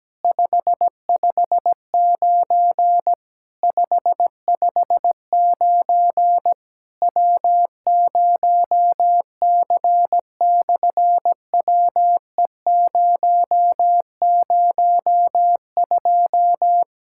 Morse code, heard as a keyed tone, about 17 wpm, 700 Hz. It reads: 559 559 W0C/WE002
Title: SOTA QSO Practice 17wpm